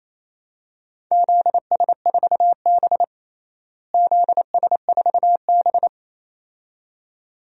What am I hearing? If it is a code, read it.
Morse code, heard as a keyed tone, about 28 wpm, 700 Hz. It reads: ZS4B ZS4B
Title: DXCC-100 Call Signs - 4 Characters - Repeated 2x - 28wpm